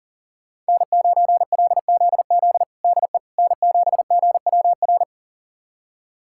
Morse code, heard as a keyed tone, about 40 wpm, 700 Hz. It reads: N9LZZ DE N7GWR
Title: USA Callsign DE Callsign - 5 Characters - 40wpm